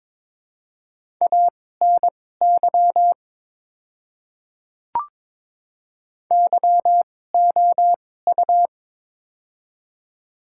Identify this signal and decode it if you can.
Morse code, heard as a keyed tone, about 22 wpm, spaced as for 16 wpm, 700 Hz. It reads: ANY YOU
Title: 2x Character Spacing - Top 200 Words - 22wpm